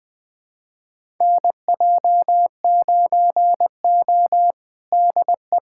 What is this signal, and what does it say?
Morse code, heard as a keyed tone, about 20 wpm, 700 Hz. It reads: NJ9O DE
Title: USA Callsign DE Callsign - 4 Characters - 20wpm